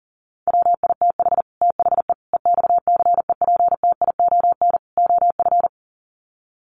Morse code, heard as a keyed tone, about 40 wpm, 700 Hz. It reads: WITH THE EXCEPTION OF
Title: Sets of 4 Words 40wpm